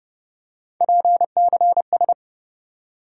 Morse code, heard as a keyed tone, about 30 wpm, 700 Hz. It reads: PCS
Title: USA Call Sign Postfixes - 3 Letters 30wpm